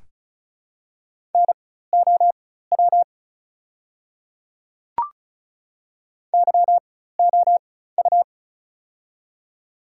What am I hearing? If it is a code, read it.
Morse code, heard as a keyed tone, about 36 wpm, spaced as for 16 wpm, 700 Hz. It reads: NOW YOU
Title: 4x Character Spacing - Top 200 Words - 35wpm